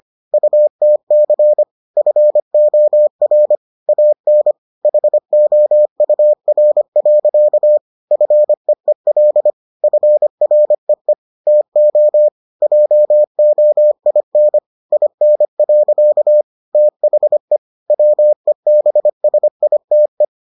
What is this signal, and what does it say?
Morse code, heard as a keyed tone, about 25 wpm, 600 Hz. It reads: UTC FOR AN HOUR. FEEL FREE TO JOIN IN. THE WEBSITE